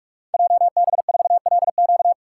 Morse code, heard as a keyed tone, about 44 wpm, 700 Hz. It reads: JB4LQ
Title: Code Groups - 5 Letter:Numbers - 45wpm